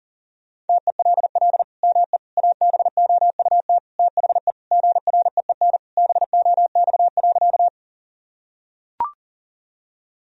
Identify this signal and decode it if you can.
Morse code, heard as a keyed tone, about 40 wpm, 700 Hz. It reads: TELL ME ABOUT THE GREEN BOX.
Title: Sentences from Top 1000 Words - Review Part 2 - 40wpm